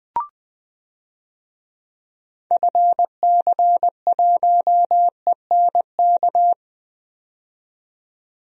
Morse code, heard as a keyed tone, about 20 wpm, 700 Hz. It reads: FC1ENK
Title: DXCC-100 Call Signs - 6 Characters - 20wpm